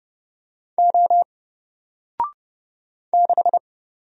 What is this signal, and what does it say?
Morse code, heard as a keyed tone, about 29 wpm, 700 Hz. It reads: O 6
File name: Single-Letter-Number-30wpm